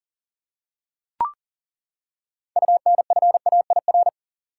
Morse code, heard as a keyed tone, about 40 wpm, 700 Hz. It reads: UNFAIR
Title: Six Letter Words - 40wpm